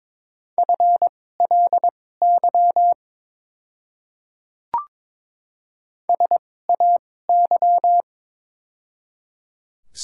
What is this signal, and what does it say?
Morse code, heard as a keyed tone, about 22 wpm, spaced as for 16 wpm, 690 Hz. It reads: FLY SAY
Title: 2x Character Spacing - Top 500 Words - 22wpm